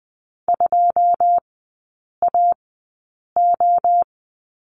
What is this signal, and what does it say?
Morse code, heard as a keyed tone, about 20 wpm, spaced as for 8 wpm, 700 Hz. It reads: 2AO
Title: 087 Lesson 15e - Number 5 - Three Characters 20wpm